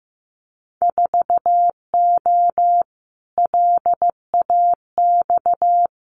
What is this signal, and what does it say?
Morse code, heard as a keyed tone, about 15 wpm, 700 Hz. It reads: 4O LAX
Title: Winter Field Day Exchanges Encore 15wpm